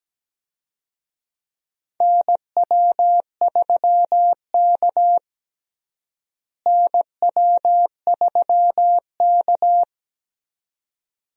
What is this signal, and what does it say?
Morse code, heard as a keyed tone, about 17 wpm, 700 Hz. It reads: NW3K NW3K
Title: USA Call Signs - 4 Characters - Repeated 2x - 17wpm